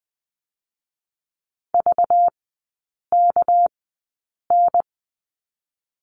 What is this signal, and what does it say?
Morse code, heard as a keyed tone, about 20 wpm, 700 Hz. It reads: V K N
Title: 198 Lesson 32e - Letter k - Three Characters 20wpm